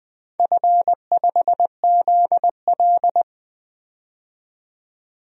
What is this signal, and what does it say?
Morse code, heard as a keyed tone, about 20 wpm, 700 Hz. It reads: F5ZL
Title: DXCC-100 Call Signs - 4 Characters - 20wpm